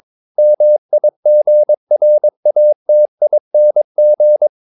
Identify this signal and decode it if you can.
Morse code, heard as a keyed tone, about 22 wpm, 600 Hz. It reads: MIGRATING
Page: Romanen Never af Ken Follet på engelsk.